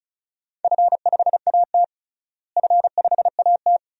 Morse code, heard as a keyed tone, about 36 wpm, 700 Hz. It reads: F5AT F5AT
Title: DXCC-100 Call Signs - 4 Characters - Repeated 2x - 35wpm